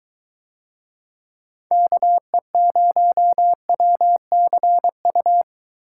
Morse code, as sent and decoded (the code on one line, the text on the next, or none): -.- . ----- .-- -.-. ..-
KE0WCU